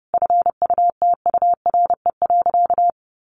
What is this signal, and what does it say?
Morse code, heard as a keyed tone, about 30 wpm, 700 Hz. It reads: FUTURE.
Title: Sentences from Top 1200 Words 30wpm